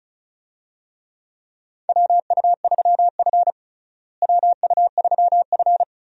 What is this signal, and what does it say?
Morse code, heard as a keyed tone, about 36 wpm, 700 Hz. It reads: WU3F WU3F
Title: USA Call Signs - 4 Characters - Repeated 3x - 35wpm